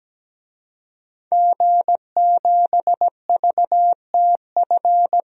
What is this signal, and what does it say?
Morse code, heard as a keyed tone, about 17 wpm, 700 Hz. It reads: G7VTF
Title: DXCC-100 Call Signs - 5 Characters - 17wpm